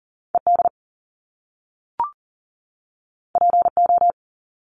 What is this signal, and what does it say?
Morse code, heard as a keyed tone, about 39 wpm, 700 Hz. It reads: ED PO
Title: Top Two Letter Combinations in Words 40wpm